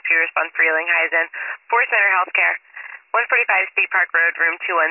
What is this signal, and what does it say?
Emergency Alert System (SAME) tones, heard as harsh buzzing bursts, with speech. It is heard from 1.0 to 4.6 s.